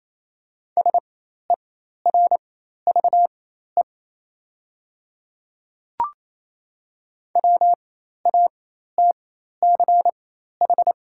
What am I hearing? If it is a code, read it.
Morse code, heard as a keyed tone, about 28 wpm, spaced as for 13 wpm, 700 Hz. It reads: SERVE WATCH
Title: 4x Character Spacing - Top 500 Words - 28wpm